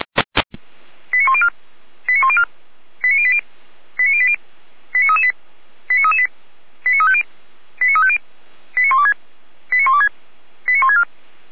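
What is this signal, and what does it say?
Five-tone selective call (ZVEI), heard as quick runs of five tones, tones of about 70 ms. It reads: 89104 89104 89080 89080 89338 89338 89350 89350 89126 89126 89164
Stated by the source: ZVEI five-tone call received by SDR.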